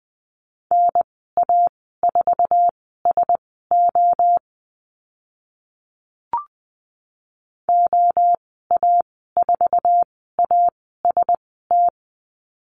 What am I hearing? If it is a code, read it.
Morse code, heard as a keyed tone, about 20 wpm, spaced as for 14 wpm, 700 Hz. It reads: NA4SO OA4AST
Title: 048 Lesson 9g - Number 4 - Call Signs 20wpm